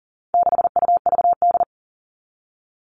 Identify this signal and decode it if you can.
Morse code, heard as a keyed tone, about 40 wpm, 700 Hz. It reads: 6UVD
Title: All Cut Numbers - 4-Digit 40wpm